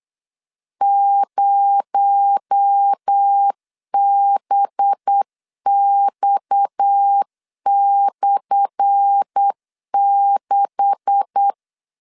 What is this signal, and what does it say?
Morse code, heard as a keyed tone, about 8 wpm, 800 Hz. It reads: 0BX/6
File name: LEZIONI / 10 SIGNS / 10C DOT 8wpm 3649